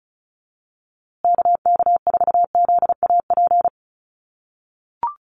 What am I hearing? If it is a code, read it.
Morse code, heard as a keyed tone, about 34 wpm, 700 Hz. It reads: KK4ZAP
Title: USA Call Signs Repeated 2x 35wpm